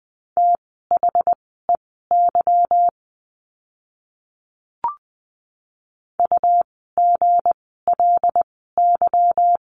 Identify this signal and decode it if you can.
Morse code, heard as a keyed tone, about 20 wpm, spaced as for 14 wpm, 700 Hz. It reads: THEY UGLY
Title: 160 Lesson 26f - Letter g - Words 20wpm